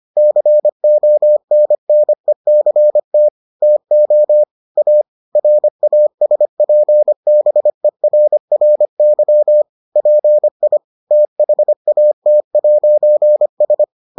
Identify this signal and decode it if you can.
Morse code, heard as a keyed tone, about 25 wpm, 600 Hz. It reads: CONNECT TO A RASPBERRY PI THAT'S